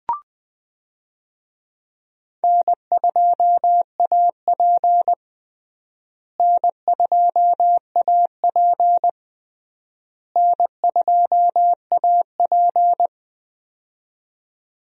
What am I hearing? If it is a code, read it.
Morse code, heard as a keyed tone, about 21 wpm, 700 Hz. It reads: N2AP N2AP N2AP
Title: USA Call Signs - 4 Characters - Repeated 3x - 20wpm